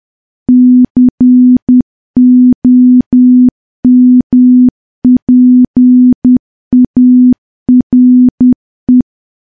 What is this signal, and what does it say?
Morse code, heard as a keyed tone, about 10 wpm, 250 Hz. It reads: COMPARE
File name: morse / morse-training-10WPM